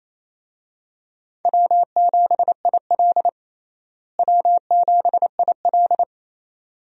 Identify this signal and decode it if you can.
Morse code, heard as a keyed tone, about 28 wpm, 700 Hz. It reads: W7IL W7IL
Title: USA Call Signs - 4 Characters - Repeated 2x - 28wpm